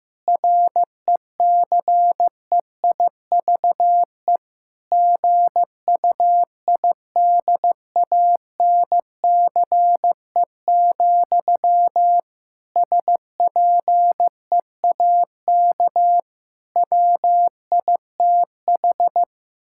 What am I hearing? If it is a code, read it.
Morse code, heard as a keyed tone, about 16 wpm, 700 Hz. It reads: RECEIVE GUIDANCE, SPEAK WITH
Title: Sentences from Top 1600 Words 15wpm